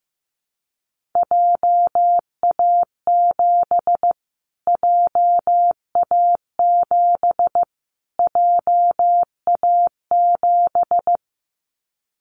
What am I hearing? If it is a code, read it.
Morse code, heard as a keyed tone, about 15 wpm, 700 Hz. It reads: JA7 JA7 JA7
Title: Callsign Prefixes for Top 25 DXCC - Repeated 3x - 15wpm